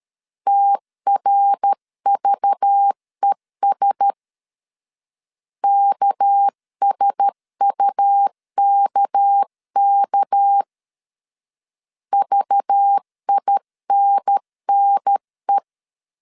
Morse code, heard as a keyed tone, about 13 wpm, 800 Hz. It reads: TRVES KSUKK VINNE
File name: CW / DET / 1 6 1500C S C 12wpm 3533